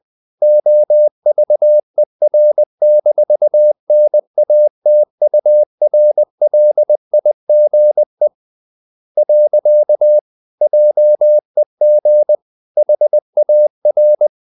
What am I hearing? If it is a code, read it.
Morse code, heard as a keyed tone, about 20 wpm, 600 Hz. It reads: OVER-NATURLIGE. JEG HAR
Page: Ferie 20 wpm